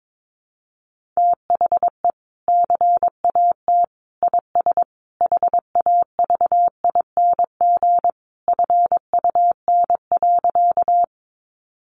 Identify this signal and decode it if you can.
Morse code, heard as a keyed tone, about 22 wpm, 700 Hz. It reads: THE CAT IS HAVING FUN.
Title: Easy Phrases 22wpm